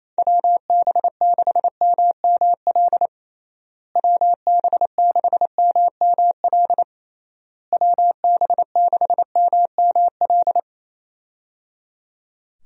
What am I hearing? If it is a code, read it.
Morse code, heard as a keyed tone, about 27 wpm, 700 Hz. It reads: WB6MML WB6MML WB6MML
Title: USA Call Signs - 6 Characters - Repeated 3x - 28wpm